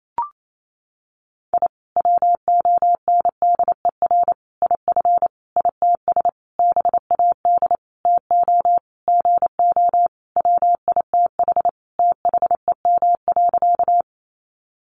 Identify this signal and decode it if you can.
Morse code, heard as a keyed tone, about 28 wpm, 700 Hz. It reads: I WONDER IF ITS BAD TO GO WITH THEM.
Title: Sentences from Top 700 Words 28wpm